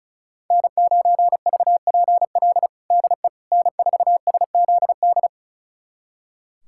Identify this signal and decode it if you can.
Morse code, heard as a keyed tone, about 34 wpm, 700 Hz. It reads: N9VPL DE N4SZD
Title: USA Callsign DE Callsign - 5 Characters - 35wpm